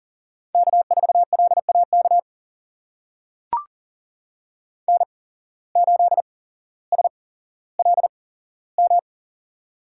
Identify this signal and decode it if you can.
Morse code, heard as a keyed tone, about 39 wpm, spaced as for 17 wpm, 700 Hz. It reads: K4LAK N8SLM
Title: USA Call Signs - 5 Characters - 8x Character Spacing - 40wpm